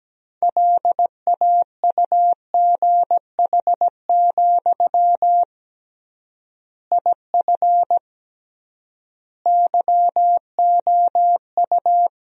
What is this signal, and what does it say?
Morse code, heard as a keyed tone, about 17 wpm, 700 Hz. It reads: LAUGH, IF YOU
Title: 3x Word-Spacing - Sentences from Top 500 Words - 17wpm